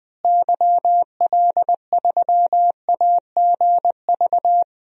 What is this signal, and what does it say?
Morse code, heard as a keyed tone, about 20 wpm, 700 Hz. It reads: YL3AGV
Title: DXCC-100 Call Signs - 6 Characters - Repeated 2x - 20wpm